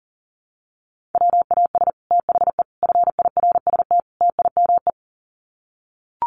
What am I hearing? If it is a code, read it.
Morse code, heard as a keyed tone, about 37 wpm, 700 Hz. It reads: WAS THE FIRST TIME
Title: Sets of 4 Words 40wpm